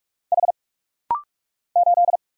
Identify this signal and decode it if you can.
Morse code, heard as a keyed tone, about 46 wpm, 700 Hz. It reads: H 8
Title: Single Letter-Number Alt - Rapid-Fire 45wpm